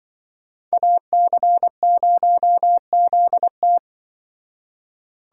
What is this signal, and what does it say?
Morse code, heard as a keyed tone, about 24 wpm, 700 Hz. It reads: AC0ZT
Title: Speed Racing - USA 2x2 Call Signs 20wpm